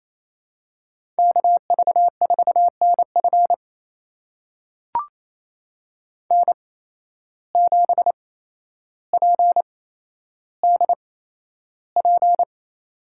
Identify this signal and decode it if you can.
Morse code, heard as a keyed tone, about 28 wpm, spaced as for 12 wpm, 700 Hz. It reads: KV4NF N7PDP
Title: USA Call Signs - 5 Characters - 8x Character Spacing - 28wpm